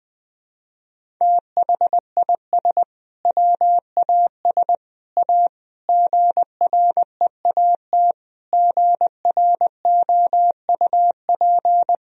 Morse code, heard as a keyed tone, about 20 wpm, 700 Hz. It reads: THIS WAS A GREAT GROUP
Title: Sentences from Top 1000 Words - Review Part 1 - 20wpm